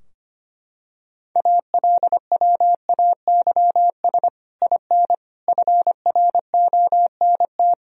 Morse code, heard as a keyed tone, about 25 wpm, 700 Hz. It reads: ALWAYS IN FRONT
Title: Sentences from Top 1000 Words - Review Part 2 - 25wpm